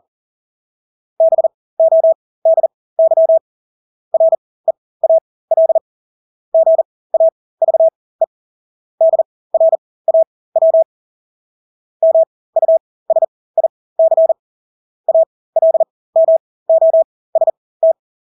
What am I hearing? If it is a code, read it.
Morse code, heard as a keyed tone, about 40 wpm, spaced as for 20 wpm, 650 Hz. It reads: BODY REAL GAVE DRAW MUSIC ALMOST
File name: Engelske ord 40-20 wpm